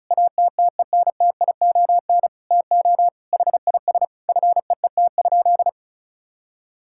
Morse code, heard as a keyed tone, about 34 wpm, 700 Hz. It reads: ATTENTION TO HIS FEET?
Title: Sentences from Top 1000 Words - Review Part 6 - 35wpm